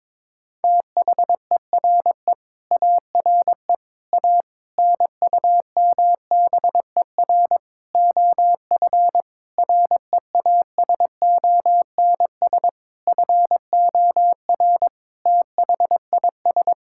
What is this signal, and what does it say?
Morse code, heard as a keyed tone, about 22 wpm, 700 Hz. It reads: THERE ARE A NUMBER OF REASONS FOR THIS
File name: practice / 22wpm / Sets of 8 Words - 22wpm